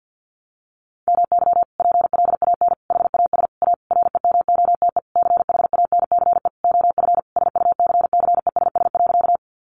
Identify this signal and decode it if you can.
Morse code, heard as a keyed tone, about 50 wpm, 700 Hz. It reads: MY PLAN HAS A REMOTE CHANCE OF SUCCESS.
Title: Sentences from Top 1300 Words 50wpm